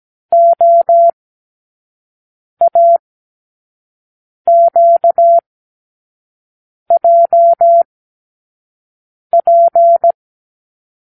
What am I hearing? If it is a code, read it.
Morse code, heard as a keyed tone, about 17 wpm, spaced as for 5 wpm, 680 Hz. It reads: OAQJP
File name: k7qo / Lesson 36 (A-X)